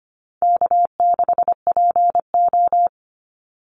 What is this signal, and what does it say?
Morse code, heard as a keyed tone, about 25 wpm, 700 Hz. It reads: K6PO
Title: Easy Call Signs - 25wpm